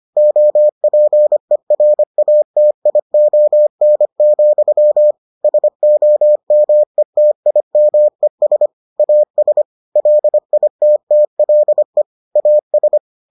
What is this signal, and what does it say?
Morse code, heard as a keyed tone, about 25 wpm, 600 Hz. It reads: OPERATION, SOMETIMES AS LITTLE AS